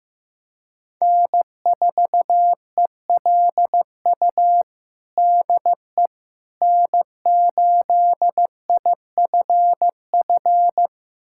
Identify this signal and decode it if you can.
Morse code, heard as a keyed tone, about 15 wpm, 700 Hz. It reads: N4ELU DE N8IFF
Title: USA Callsign DE Callsign - 5 Characters - 15wpm